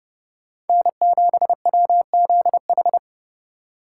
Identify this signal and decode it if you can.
Morse code, heard as a keyed tone, about 39 wpm, 700 Hz. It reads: N7WZH